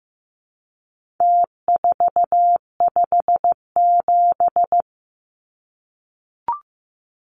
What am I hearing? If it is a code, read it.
Morse code, heard as a keyed tone, about 15 wpm, 700 Hz. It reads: T457
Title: Numbers - Contest Style - 4-Digit 15wpm